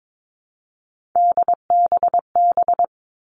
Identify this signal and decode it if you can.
Morse code, heard as a keyed tone, about 22 wpm, 700 Hz. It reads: DBB
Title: All Cut Numbers - 3-Digit 22wpm